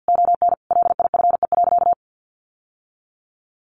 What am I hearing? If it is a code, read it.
Morse code, heard as a keyed tone, about 50 wpm, 700 Hz. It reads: ON LIFE.
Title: Sentences from Top 1200 Words 50wpm